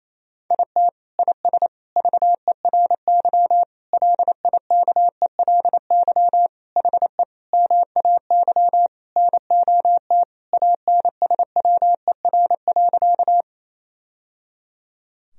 Morse code, heard as a keyed tone, about 28 wpm, 700 Hz. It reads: IT IS VERY LIKELY HE MAY NOT ANSWER.
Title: Sentences from Top 1000 Words - Review Part 4 - 28wpm